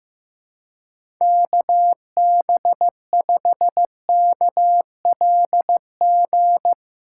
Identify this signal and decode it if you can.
Morse code, heard as a keyed tone, about 16 wpm, 700 Hz. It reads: KB5KLG
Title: USA Callsign DE Callsign 15wpm